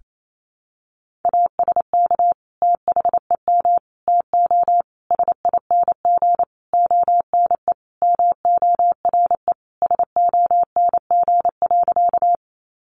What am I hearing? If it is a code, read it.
Morse code, heard as a keyed tone, about 28 wpm, 700 Hz. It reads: ASK THEM TO SING ONE MORE SONG.
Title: Sentences from Top 400 Words 28wpm